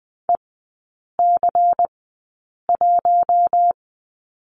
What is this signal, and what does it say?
Morse code, heard as a keyed tone, about 20 wpm, spaced as for 8 wpm, 700 Hz. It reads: EC1
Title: 100 Lesson 17e - Letter u - Three Characters 20wpm